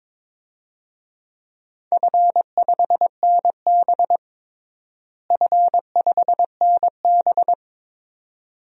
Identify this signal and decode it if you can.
Morse code, heard as a keyed tone, about 22 wpm, 700 Hz. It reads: F5NB F5NB
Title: DXCC-100 Call Signs - 4 Characters - Repeated 2x - 22wpm